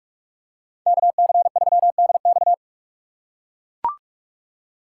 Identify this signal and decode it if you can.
Morse code, heard as a keyed tone, about 41 wpm, 700 Hz. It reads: KC3DX
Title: USA Call Signs - 5 Characters - 4x Character Spacing - 45wpm